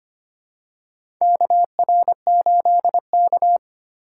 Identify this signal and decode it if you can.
Morse code, heard as a keyed tone, about 25 wpm, 700 Hz. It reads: KR8K
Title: USA Call Signs - 4 Characters - 2x Character Spacing - 25wpm